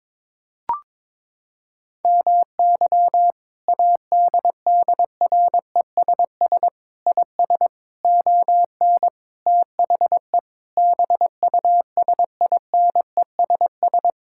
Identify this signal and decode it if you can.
Morse code, heard as a keyed tone, about 22 wpm, 700 Hz. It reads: MY ADDRESS IS ON THE BUSINESS
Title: Sentences from Top 1000 Words - Review Part 6 - 22wpm